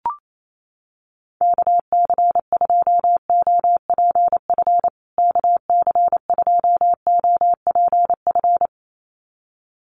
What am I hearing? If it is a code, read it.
Morse code, heard as a keyed tone, about 28 wpm, 700 Hz. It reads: KC2OPF KC2OPF
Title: USA Call Signs Repeated 2x 28wpm